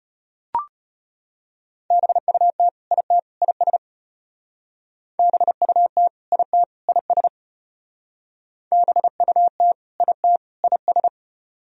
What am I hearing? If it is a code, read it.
Morse code, heard as a keyed tone, about 33 wpm, 700 Hz. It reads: BUT IT IS BUT IT IS BUT IT IS